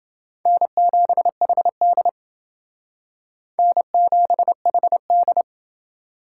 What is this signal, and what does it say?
Morse code, heard as a keyed tone, about 29 wpm, 700 Hz. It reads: N7HD N7HD